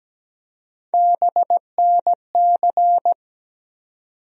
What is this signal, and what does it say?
Morse code, heard as a keyed tone, about 17 wpm, 700 Hz. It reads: BNC
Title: USA Call Sign Postfixes - 3 Letters 17wpm